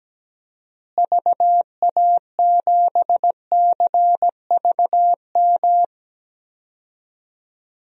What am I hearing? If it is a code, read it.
Morse code, heard as a keyed tone, about 17 wpm, 700 Hz. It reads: VA7CVM
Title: DXCC-100 Call Signs - 6 Characters - Repeated 2x - 17wpm